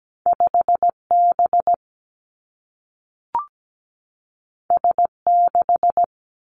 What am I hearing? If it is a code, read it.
Morse code, heard as a keyed tone, about 17 wpm, 700 Hz. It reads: 5 B S 6
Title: Two Challenging Letter-Number Combinations 17wpm